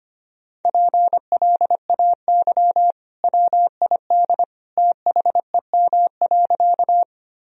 Morse code, heard as a keyed tone, about 25 wpm, 700 Hz. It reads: PLAY WID THEM.
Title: Sentences from Top 200 Words and Abbreviations 25wpm